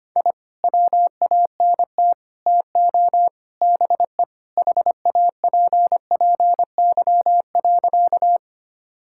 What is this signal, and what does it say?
Morse code, heard as a keyed tone, about 25 wpm, 700 Hz. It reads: I WANT TO BE HAPPY.
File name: practice / 25wpm / Sentences with 4-Word Phrases - 25wpm